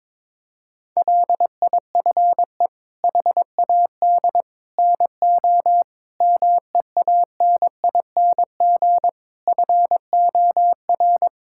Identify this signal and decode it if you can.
Morse code, heard as a keyed tone, about 22 wpm, 700 Hz. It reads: LIFE HAD NO MEANING FOR
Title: Sentences from Top 1000 Words - Review Part 6 - 22wpm